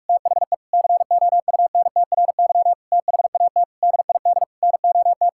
Morse code, heard as a keyed tone, about 46 wpm, 700 Hz. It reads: THE COUNTRY THAT DID NOT
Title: Sentences from Top 1600 Words 45wpm